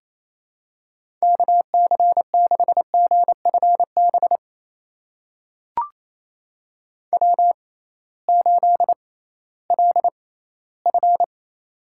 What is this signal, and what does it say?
Morse code, heard as a keyed tone, about 27 wpm, spaced as for 16 wpm, 700 Hz. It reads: KC6GFB W8LF
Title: USA Call Signs - 6x Character Spacing - 28wpm